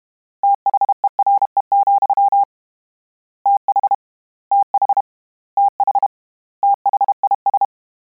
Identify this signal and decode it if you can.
Morse code, heard as a keyed tone, about 32 wpm, 800 Hz. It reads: THERE, TH TH TH THIS
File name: Combinations / Combinations-mp3s / Combinations@30